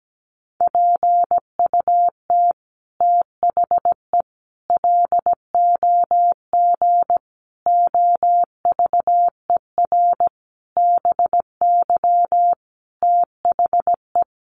Morse code, heard as a keyed tone, about 17 wpm, 700 Hz. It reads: PUT THE LOG OVER BY THE
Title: Sentences from Top 1400 Words 17wpm